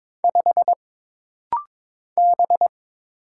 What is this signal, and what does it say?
Morse code, heard as a keyed tone, about 22 wpm, 700 Hz. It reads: 5 B
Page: This rapid-fire timing is geared to push your ICR (Instant Character Recognition) skills.